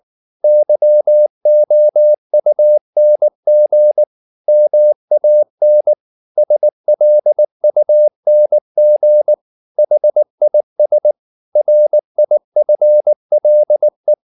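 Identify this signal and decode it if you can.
Morse code, heard as a keyed tone, about 19 wpm, 600 Hz. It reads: YOUNG MAN SLUNG HIS RIFLE
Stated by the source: Morse code at 19 wpm from CW med Gnister, reading Ken Follett's novel Never in English.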